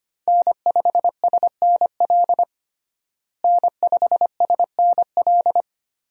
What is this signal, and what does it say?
Morse code, heard as a keyed tone, about 25 wpm, 700 Hz. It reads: N5SNL N5SNL
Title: USA Call Signs - 5 Characters - Repeated 2x - 25wpm